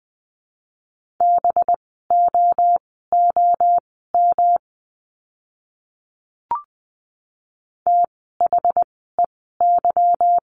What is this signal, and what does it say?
Morse code, heard as a keyed tone, about 20 wpm, spaced as for 14 wpm, 700 Hz. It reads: BOOM THEY
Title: 185 Lesson 30f - Letter b - Words 20wpm